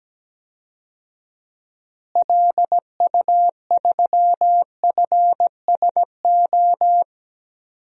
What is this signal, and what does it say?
Morse code, heard as a keyed tone, about 17 wpm, 700 Hz. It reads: LU3FSO
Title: DXCC-100 Call Signs - 6 Characters - Repeated 2x - 17wpm